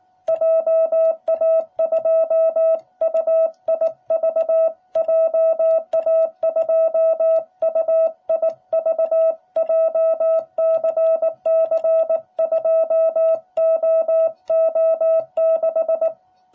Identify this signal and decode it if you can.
Morse code, heard as a keyed tone, about 19 wpm, 660 Hz. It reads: JA2UIV JA2UIV JCC2OO6